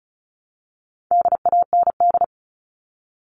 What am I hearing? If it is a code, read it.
Morse code, heard as a keyed tone, about 34 wpm, 700 Hz. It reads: DAND
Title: All Cut Numbers - 4-Digit 35wpm